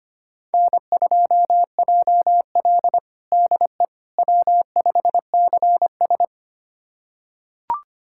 Morse code, heard as a keyed tone, about 25 wpm, 700 Hz. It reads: N2JL DE W5CS
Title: USA Callsign DE Callsign - 4 Characters - 25wpm